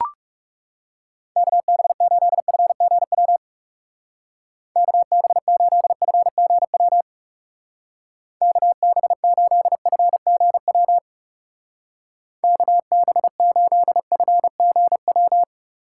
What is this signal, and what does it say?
Morse code, heard as a keyed tone, about 39 wpm, 700 Hz. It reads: KB8FGW KB8FGW KB8FGW KB8FGW